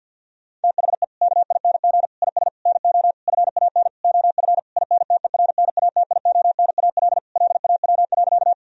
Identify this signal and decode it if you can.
Morse code, heard as a keyed tone, about 50 wpm, 700 Hz. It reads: THE KING IS NO FAN OF INTERNATIONAL LAW.
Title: Sentences from Top 1000 Words - Review Part 4 - 50wpm